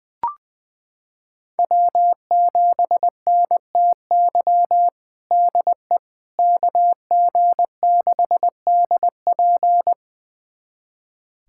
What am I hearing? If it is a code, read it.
Morse code, heard as a keyed tone, about 21 wpm, 700 Hz. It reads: W7NTY DE KG6DP
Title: USA Callsign DE Callsign - 5 Characters - 20wpm